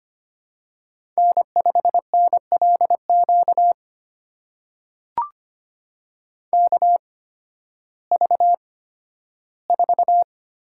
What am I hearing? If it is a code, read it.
Morse code, heard as a keyed tone, about 25 wpm, spaced as for 13 wpm, 700 Hz. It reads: N5NLQ KV4
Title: USA Call Signs - 5 Characters - 8x Character Spacing - 25wpm